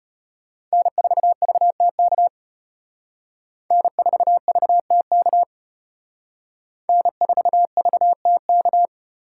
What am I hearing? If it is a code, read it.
Morse code, heard as a keyed tone, about 33 wpm, 700 Hz. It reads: N4VTK N4VTK N4VTK